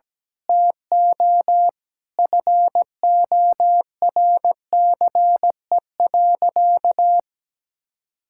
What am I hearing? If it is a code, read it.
Morse code, heard as a keyed tone, about 17 wpm, 700 Hz. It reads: TO FORCE.
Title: Sentences from Top 1000 Words - Review Part 6 - 17wpm